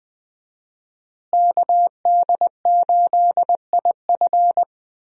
Morse code, heard as a keyed tone, about 21 wpm, 700 Hz. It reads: KD8IF
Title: USA Call Signs - 5 Characters - Repeated 2x - 20wpm